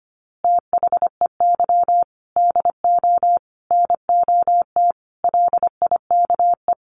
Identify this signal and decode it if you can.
Morse code, heard as a keyed tone, about 25 wpm, 700 Hz. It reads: THEY DO NOT LIKE
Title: Easy Phrases 25wpm